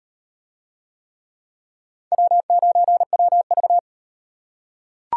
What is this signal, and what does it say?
Morse code, heard as a keyed tone, about 38 wpm, 700 Hz. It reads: W9WV